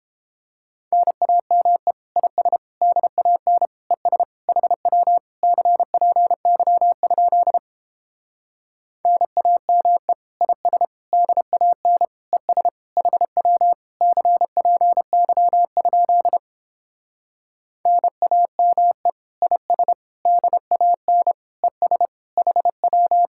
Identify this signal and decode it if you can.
Morse code, heard as a keyed tone, about 29 wpm, 700 Hz. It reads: NAME IS DAN ES HW CPY? NAME IS DAN ES HW CPY? NAME IS DAN ES HW